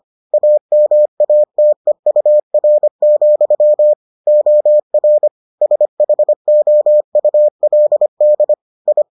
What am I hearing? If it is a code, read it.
Morse code, heard as a keyed tone, about 25 wpm, 600 Hz. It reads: AMATEUR, OR SHOULD I